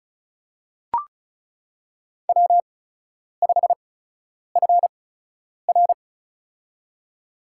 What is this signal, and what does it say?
Morse code, heard as a keyed tone, about 36 wpm, spaced as for 10 wpm, 700 Hz. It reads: W5FR
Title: USA Call Signs - 8x Character Spacing - 35wpm